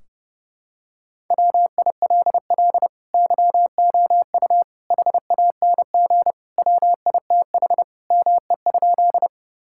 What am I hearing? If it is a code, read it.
Morse code, heard as a keyed tone, about 32 wpm, 700 Hz. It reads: WILL YOU HANG WITH ME?
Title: Sentences from Top 1600 Words 30wpm